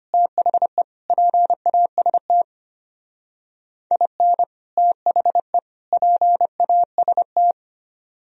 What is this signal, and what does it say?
Morse code, heard as a keyed tone, about 32 wpm, 700 Hz. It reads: THE PAST IN THE PAST